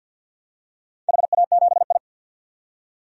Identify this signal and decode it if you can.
Morse code, heard as a keyed tone, about 48 wpm, 700 Hz. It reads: HA7I
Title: DXCC-100 Call Signs - 4 Characters - 50wpm